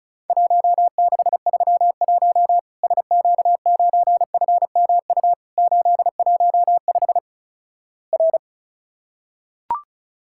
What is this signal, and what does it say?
Morse code, heard as a keyed tone, about 36 wpm, 700 Hz. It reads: 1631 SQ9FMU 815 R
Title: Worked All Europe DX Contest - QTC Exchange - 35wpm